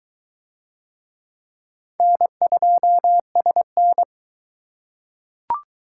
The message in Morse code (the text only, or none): N2SN